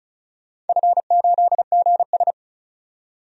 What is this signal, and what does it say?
Morse code, heard as a keyed tone, about 34 wpm, 700 Hz. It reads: F8GS
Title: DXCC-100 Call Signs - 4 Characters - 35wpm